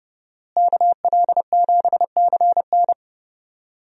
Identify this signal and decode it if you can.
Morse code, heard as a keyed tone, about 30 wpm, 700 Hz. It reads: KL7CN
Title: USA Call Signs - 5 Characters - Repeated 2x - 30wpm